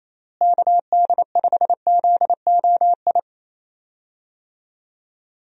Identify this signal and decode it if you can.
Morse code, heard as a keyed tone, about 27 wpm, 700 Hz. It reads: KD5ZOI
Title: USA Call Signs - 6 Characters - Repeated 3x - 28wpm